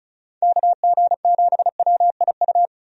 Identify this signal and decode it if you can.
Morse code, heard as a keyed tone, about 36 wpm, 700 Hz. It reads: KG7WIU
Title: USA Call Signs - 6 Characters - 35wpm